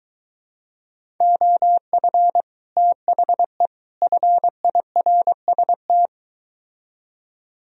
Morse code, heard as a keyed tone, about 23 wpm, 700 Hz. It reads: OF THE FIRST